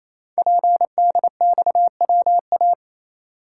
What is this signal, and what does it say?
Morse code, heard as a keyed tone, about 28 wpm, 700 Hz. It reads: PDXWA
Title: Code Groups - 5 Letter:Numbers - 28wpm